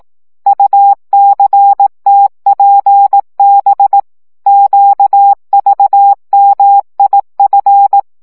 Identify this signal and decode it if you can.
Morse code, heard as a keyed tone, about 18 wpm, 800 Hz. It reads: UCTPB QVMIF
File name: morze vizsga 2011-3 15